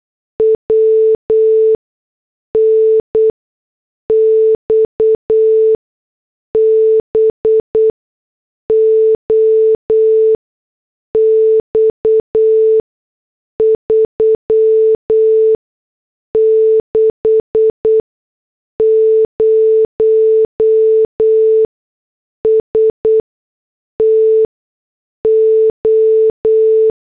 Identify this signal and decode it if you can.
Morse code, heard as a keyed tone, about 8 wpm, spaced as for 6 wpm, 440 Hz. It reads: WN XBOX 360 STO
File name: morse / headlines